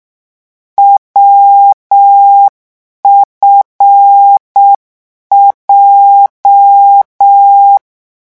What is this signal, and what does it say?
Morse code, heard as a keyed tone, about 6 wpm, 800 Hz. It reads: WFJ